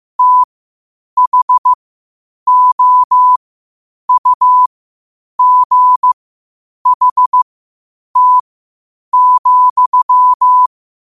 Morse code, heard as a keyed tone, about 15 wpm, spaced as for 8 wpm, 1000 Hz. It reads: THOUGHT,
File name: qotdfiles / 8wpm